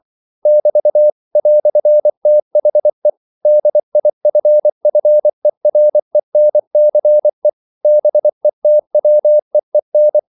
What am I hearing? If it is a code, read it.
Morse code, heard as a keyed tone, about 24 wpm, 600 Hz. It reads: <BT> "THE DIFFERENCE BETWEEN
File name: CW / Practice / EnglishQuotations0424